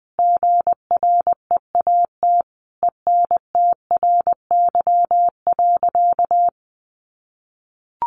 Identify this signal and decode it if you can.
Morse code, heard as a keyed tone, about 20 wpm, 700 Hz. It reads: GREAT ENTRY.
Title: Sentences from Top 1500 Words 20wpm